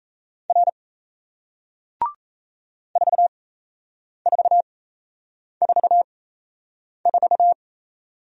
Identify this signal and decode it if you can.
Morse code, heard as a keyed tone, about 38 wpm, spaced as for 8 wpm, 700 Hz. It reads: R 4444